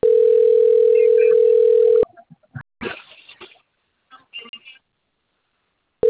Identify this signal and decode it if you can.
Ringback tone (North America), 440 + 480 Hz.